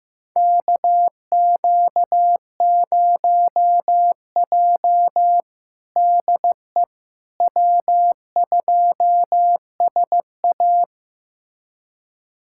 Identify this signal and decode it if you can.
Morse code, heard as a keyed tone, about 15 wpm, 700 Hz. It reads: KQ0J DE W2SA
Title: USA Callsign DE Callsign - 4 Characters - 15wpm